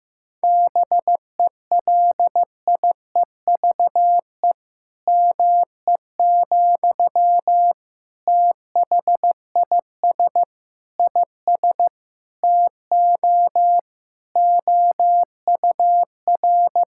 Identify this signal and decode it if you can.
Morse code, heard as a keyed tone, about 15 wpm, 700 Hz. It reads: BELIEVE ME, THIS IS TO OUR
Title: Sentences from Top 1000 Words - Review Part 6 - 15wpm